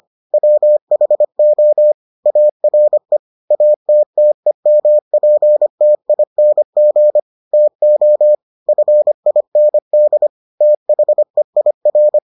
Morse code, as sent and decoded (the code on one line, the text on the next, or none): .-- .... --- / .- .-. / .- - - . -- .--. - .. -. --. / - --- / ..-. .. -. -.. / - .... .. .-.
WHO ARE ATTEMPTING TO FIND THEIR